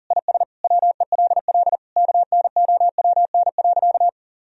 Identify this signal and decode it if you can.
Morse code, heard as a keyed tone, about 40 wpm, 700 Hz. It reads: IS WELL KNOWN.
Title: Sentences from Top 1000 Words - Review Part 4 - 40wpm